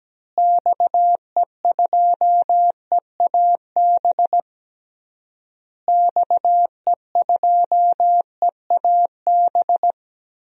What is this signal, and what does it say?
Morse code, heard as a keyed tone, about 17 wpm, 700 Hz. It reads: XE2EAB XE2EAB
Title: DXCC-100 Call Signs - 6 Characters - Repeated 3x - 17wpm